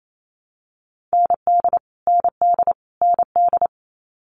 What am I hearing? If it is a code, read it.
Morse code, heard as a keyed tone, about 28 wpm, 700 Hz. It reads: ND ND ND
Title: Multi-letter Phonemes Repeated 3x 28wpm